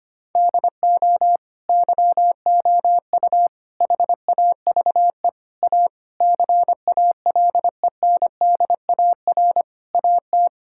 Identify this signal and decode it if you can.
Morse code, heard as a keyed tone, about 25 wpm, 700 Hz. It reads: DO YOU HAVE A CALENDAR AT
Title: Sentences from Top 1600 Words 25wpm